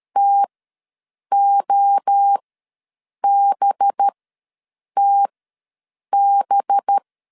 Morse code, heard as a keyed tone, about 13 wpm, spaced as for 7 wpm, 800 Hz. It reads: TOBTB
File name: 5 LB ws 12wpm